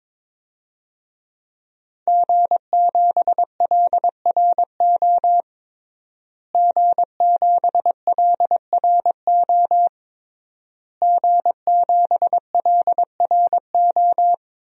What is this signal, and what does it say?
Morse code, heard as a keyed tone, about 22 wpm, 700 Hz. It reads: G7LRO G7LRO G7LRO
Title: DXCC-100 Call Signs - 5 Characters - Repeated 3x - 22wpm